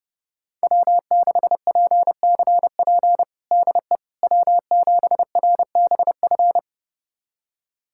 Morse code, heard as a keyed tone, about 32 wpm, 700 Hz. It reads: W6PCP DE W7RBF
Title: USA Callsign DE Callsign - 5 Characters - 30wpm